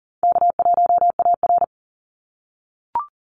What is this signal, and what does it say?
Morse code, heard as a keyed tone, about 40 wpm, 700 Hz. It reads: K1AR
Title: Easy Call Signs - 40wpm